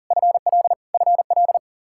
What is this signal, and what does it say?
Morse code, heard as a keyed tone, about 40 wpm, 700 Hz. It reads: FL FL
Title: K1USN SST QSO Practice 40wpm